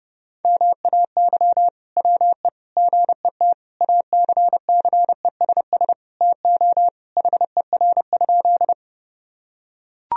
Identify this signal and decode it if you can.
Morse code, heard as a keyed tone, about 30 wpm, 700 Hz. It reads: MAY WE GET ACCESS TO HER?
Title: Sentences from Top 1000 Words - Review Part 4 - 30wpm